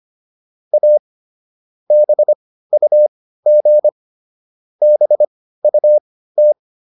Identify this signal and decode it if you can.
Morse code, heard as a keyed tone, about 25 wpm, spaced as for 15 wpm, 600 Hz. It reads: A BUG BUT
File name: SS215 15